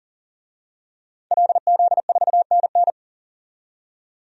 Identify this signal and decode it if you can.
Morse code, heard as a keyed tone, about 40 wpm, 700 Hz. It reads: LZ4NN
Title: DXCC-100 Call Signs - 5 Characters - 40wpm